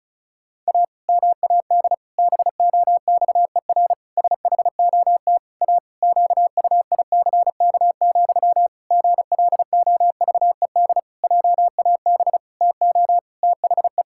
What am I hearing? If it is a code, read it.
Morse code, heard as a keyed tone, about 34 wpm, 700 Hz. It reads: A MAD BOXER SHOT A QUICK, GLOVED JAB TO THE
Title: Pangram Challenge v2 - 35wpm